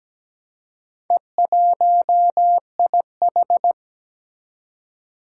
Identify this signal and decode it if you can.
Morse code, heard as a keyed tone, about 17 wpm, 700 Hz. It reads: E1IH
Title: Four Challenging Letter-Number Combinations 17wpm